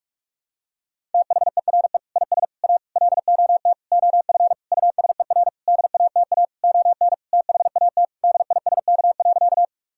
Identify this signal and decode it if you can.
Morse code, heard as a keyed tone, about 41 wpm, 700 Hz. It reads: THERE IS A LOT OF USER DATA ON THAT DISK.
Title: Sentences from Top 1000 Words - Review Part 6 - 45wpm